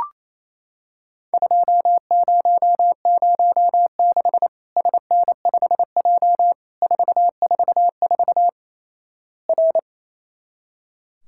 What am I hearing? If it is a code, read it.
Morse code, heard as a keyed tone, about 28 wpm, 700 Hz. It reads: 2006 SN5J 444 R
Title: Worked All Europe DX Contest - QTC Exchange - 28wpm